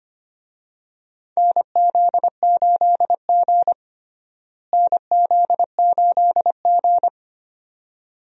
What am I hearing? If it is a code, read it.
Morse code, heard as a keyed tone, about 25 wpm, 700 Hz. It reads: NZ8G NZ8G
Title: USA Call Signs - 4 Characters - Repeated 2x - 25wpm